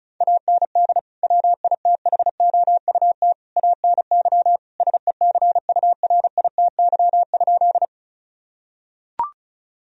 Morse code, heard as a keyed tone, about 34 wpm, 700 Hz. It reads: AND WITHOUT ANY SECURITY?
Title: Sentences from Top 1000 Words - Review Part 4 - 35wpm